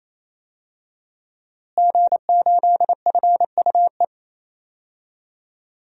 Morse code, heard as a keyed tone, about 27 wpm, 700 Hz. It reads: G8FUE
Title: DXCC-100 Call Signs - 5 Characters - Repeated 3x - 28wpm